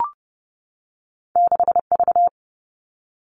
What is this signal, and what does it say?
Morse code, heard as a keyed tone, about 30 wpm, 700 Hz. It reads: 6V
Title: Two Challenging Letter-Number Combinations 30wpm